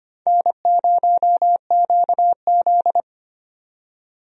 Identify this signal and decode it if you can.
Morse code, heard as a keyed tone, about 25 wpm, 700 Hz. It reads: N0QZ
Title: USA Call Signs - 4 Characters - 25wpm